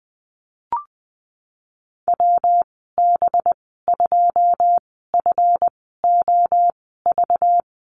Morse code, heard as a keyed tone, about 20 wpm, spaced as for 15 wpm, 700 Hz. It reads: WB2FOV
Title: 193 Lesson 31g - Letter v - Call Signs 20wpm